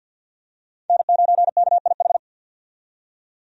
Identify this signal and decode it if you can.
Morse code, heard as a keyed tone, about 49 wpm, 700 Hz. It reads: N9KIH
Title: USA Call Signs - 5 Characters - 50wpm